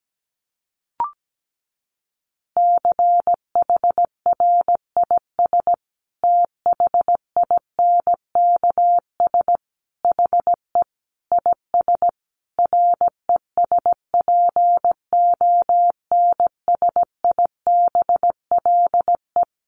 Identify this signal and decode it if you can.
Morse code, heard as a keyed tone, about 17 wpm, 700 Hz. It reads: CHRIS THINKS HE IS RESPONSIBLE
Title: Additional Easy Phrases 17wpm